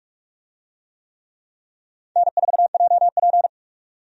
Morse code, heard as a keyed tone, about 46 wpm, 700 Hz. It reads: N4JP
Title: USA Call Signs - 4 Characters - 45wpm